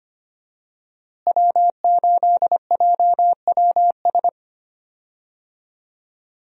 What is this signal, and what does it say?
Morse code, heard as a keyed tone, about 25 wpm, 700 Hz. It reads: W8JWS
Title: USA Call Signs - 5 Characters - Repeated 2x - 25wpm